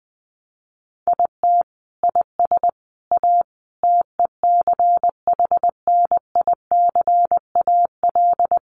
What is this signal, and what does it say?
Morse code, heard as a keyed tone, about 20 wpm, 700 Hz. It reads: IT IS A TECHNICAL
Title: Sentences from Top 1100 Words 20wpm